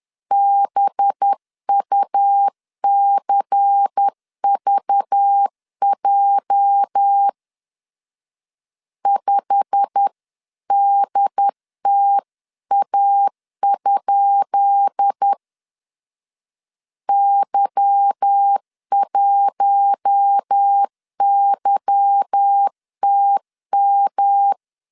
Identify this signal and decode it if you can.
Morse code, heard as a keyed tone, about 10 wpm, 800 Hz. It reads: BUCVJ 5DTA? Y1YTM
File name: LEZIONI / 10 SIGNS / 10B INT 10wpm 2844